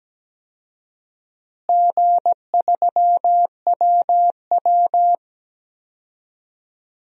Morse code, heard as a keyed tone, about 17 wpm, 700 Hz. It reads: G3WW
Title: DXCC-100 Call Signs - 4 Characters - Repeated 3x - 17wpm